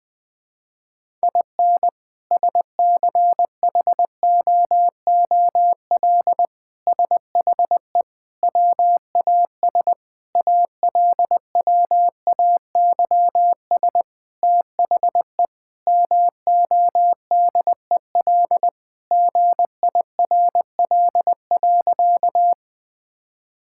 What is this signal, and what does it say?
Morse code, heard as a keyed tone, about 20 wpm, 700 Hz. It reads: IN SCHOOL SHE WAS ALWAYS THE MODEL GIRL.
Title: Sentences from Top 1000 Words - Review Part 5 - 20wpm